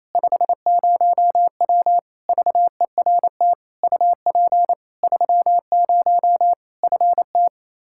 Morse code, heard as a keyed tone, about 28 wpm, 700 Hz. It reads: 50W VERT UP 30 FT
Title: Practice QSO - Exchange 2 - Info - 28wpm